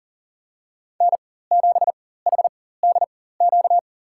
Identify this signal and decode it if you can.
Morse code, heard as a keyed tone, about 40 wpm, spaced as for 19 wpm, 700 Hz. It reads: N7HDQ
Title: USA Call Signs - 5 Characters - 4x Character Spacing - 40wpm